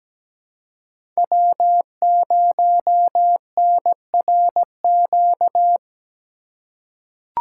Morse code, heard as a keyed tone, about 17 wpm, 700 Hz. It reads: W0NRQ
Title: USA Call Signs - 5 Characters - 2x Character Spacing - 17wpm